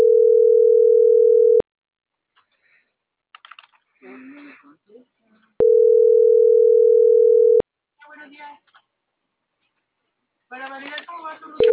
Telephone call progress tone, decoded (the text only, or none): ringback tone